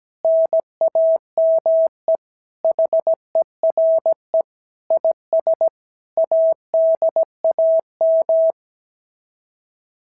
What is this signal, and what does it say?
Morse code, heard as a keyed tone, about 16 wpm, 650 Hz. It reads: NAME HERE IS ADAM
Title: SOTA QSO Practice 17wpm